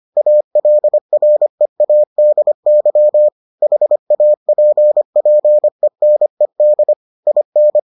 Morse code, heard as a keyed tone, about 25 wpm, 600 Hz. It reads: ALREADY HAPPENED IN